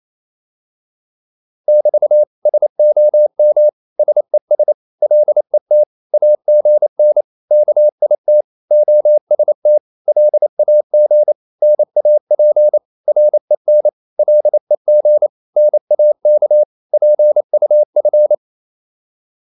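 Morse code, heard as a keyed tone, about 28 wpm, 600 Hz. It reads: <BT> SOM SES LET AGN KIT OST LAG NAP REN LEG NAK PUF
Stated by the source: Danish short words sent in Morse at 28 wpm, from CW med Gnister.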